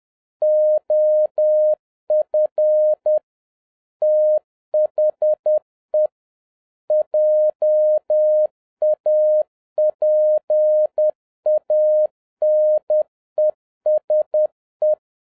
Morse code, heard as a keyed tone, about 10 wpm, 610 Hz. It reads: OF THE JAPANESE
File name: 2011-07-mp3 / n21 GREETINGS FROM SINGAPORE 10wpm